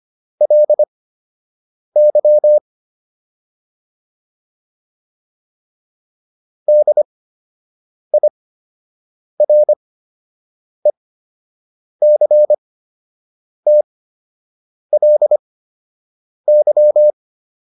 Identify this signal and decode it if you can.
Morse code, heard as a keyed tone, about 25 wpm, spaced as for 7 wpm, 600 Hz. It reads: LY DIRECTLY